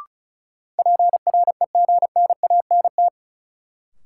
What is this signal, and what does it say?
Morse code, heard as a keyed tone, about 36 wpm, 700 Hz. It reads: PREGNANT
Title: Sentences from Top 1000 Words - Review Part 6 - New Word Drill - 35wpm